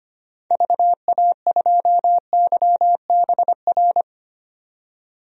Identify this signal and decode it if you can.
Morse code, heard as a keyed tone, about 25 wpm, 700 Hz. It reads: VA2YBR
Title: DXCC-100 Call Signs - 6 Characters - 25wpm